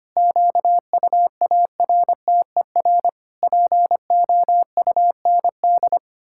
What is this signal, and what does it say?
Morse code, heard as a keyed tone, about 25 wpm, 700 Hz. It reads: QUARTER POUND
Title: Pangram Challenge v2 - 25wpm